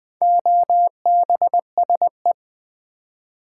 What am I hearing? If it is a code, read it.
Morse code, heard as a keyed tone, about 20 wpm, 700 Hz. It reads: OBSE
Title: Build A Word - 2 Letters At A Time - 8-Letter Words - 20wpm